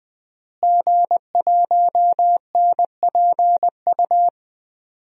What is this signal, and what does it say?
Morse code, heard as a keyed tone, about 20 wpm, 700 Hz. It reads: G1NPU
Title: DXCC-100 Call Signs - 5 Characters - Repeated 2x - 20wpm